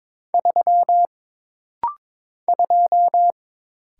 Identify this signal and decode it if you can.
Morse code, heard as a keyed tone, about 22 wpm, 700 Hz. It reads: 3 2
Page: This practice set is composed of Letters, Numbers, common punctuations, and common prosigns.